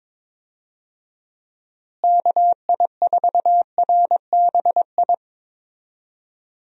Morse code, heard as a keyed tone, about 22 wpm, 700 Hz. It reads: KI4RBI
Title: USA Call Signs - 6 Characters - Repeated 2x - 22wpm